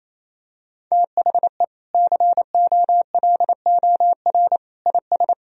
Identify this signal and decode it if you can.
Morse code, heard as a keyed tone, about 28 wpm, 700 Hz. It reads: THE COLOR IS
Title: Sentences from Top 1000 Words - Review Part 2 - 28wpm